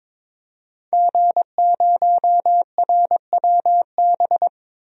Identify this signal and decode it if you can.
Morse code, heard as a keyed tone, about 22 wpm, 700 Hz. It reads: G0RWB
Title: DXCC-100 Call Signs - 5 Characters - 22wpm